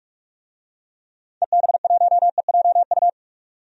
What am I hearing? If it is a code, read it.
Morse code, heard as a keyed tone, about 46 wpm, 700 Hz. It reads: EB1EJU
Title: DXCC-100 Call Signs - 6 Characters - 45wpm